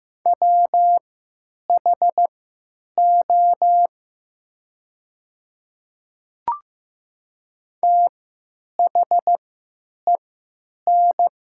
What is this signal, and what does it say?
Morse code, heard as a keyed tone, about 15 wpm, spaced as for 8 wpm, 700 Hz. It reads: WHO THEN
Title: 3x Character Spacing - Top 100 Words - 15wpm